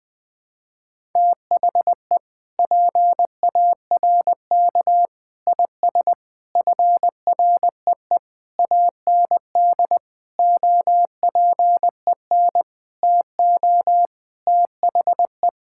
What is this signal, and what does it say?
Morse code, heard as a keyed tone, about 20 wpm, 700 Hz. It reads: THE PARK IS FREE AND OPEN TO THE
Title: Sentences with 7-Word Phrases - 20wpm